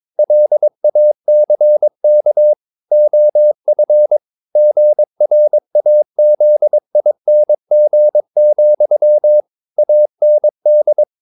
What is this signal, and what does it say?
Morse code, heard as a keyed tone, about 22 wpm, 600 Hz. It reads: LACK OF GRAZING, AND
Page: Romanen Never af Ken Follet på engelsk.